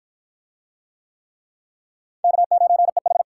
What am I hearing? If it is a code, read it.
Morse code, heard as a keyed tone, about 48 wpm, 700 Hz. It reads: K9EH